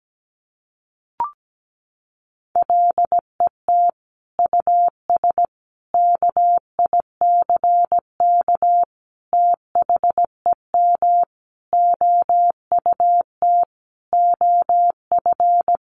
Morse code, heard as a keyed tone, about 17 wpm, 700 Hz. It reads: LET US KICK THEM OUT OF
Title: Sentences from Top 1300 Words 17wpm